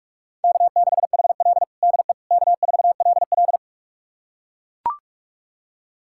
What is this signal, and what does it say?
Morse code, heard as a keyed tone, about 41 wpm, 700 Hz. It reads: K6HL DE K4LL
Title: USA Callsign DE Callsign - 4 Characters - 45wpm